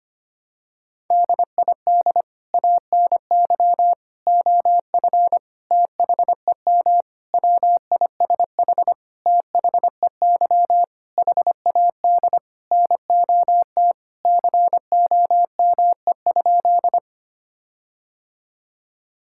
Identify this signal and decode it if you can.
Morse code, heard as a keyed tone, about 25 wpm, 700 Hz. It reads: DID ANY OF THEM WISH THEY HAD NOT COME?
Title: Sentences from Top 1600 Words 25wpm